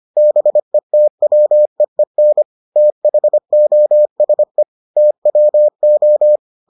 Morse code, heard as a keyed tone, about 25 wpm, 600 Hz. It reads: BETWEEN THOSE TWO